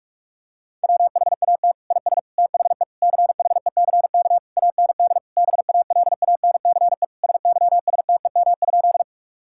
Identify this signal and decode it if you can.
Morse code, heard as a keyed tone, about 46 wpm, 700 Hz. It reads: WHAT IS THE CHECK AND BALANCE SYSTEM?
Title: Sentences from Top 1000 Words - Review Part 6 - 45wpm